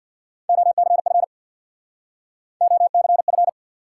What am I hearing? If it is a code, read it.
Morse code, heard as a keyed tone, about 69 wpm, 700 Hz. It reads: OCF OCF